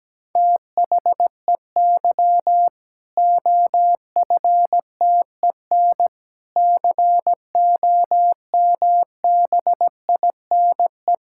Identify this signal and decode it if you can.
Morse code, heard as a keyed tone, about 17 wpm, 700 Hz. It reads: THEY OFTEN COMBINE
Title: Sentences from Top 1600 Words 17wpm